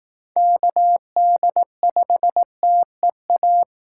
Morse code, heard as a keyed tone, about 19 wpm, 700 Hz. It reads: KD5TEA
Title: Speed Racing - USA 2x3 Call Signs 15wpm